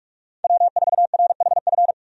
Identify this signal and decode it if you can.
Morse code, heard as a keyed tone, about 47 wpm, 700 Hz. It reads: W4RHF
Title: USA Call Signs - 5 Characters - Repeated 3x - 45wpm